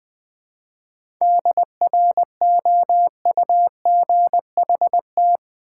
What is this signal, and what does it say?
Morse code, heard as a keyed tone, about 20 wpm, 700 Hz. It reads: DROUGHT
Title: 7 Letter Words - First 3 Letters Given - 20wpm